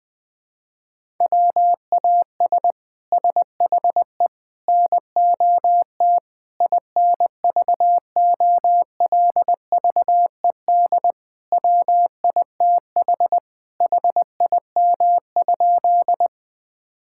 Morse code, heard as a keyed tone, about 21 wpm, 700 Hz. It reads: WAS SHE NOT INVOLVED WITH HIM?
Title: Sentences from Top 1000 Words - Review Part 5 - 20wpm